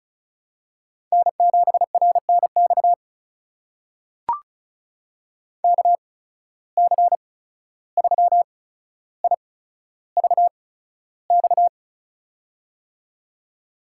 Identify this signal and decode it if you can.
Morse code, heard as a keyed tone, about 36 wpm, spaced as for 14 wpm, 700 Hz. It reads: N7RNX KC3IVX
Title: USA Call Signs - 8x Character Spacing - 35wpm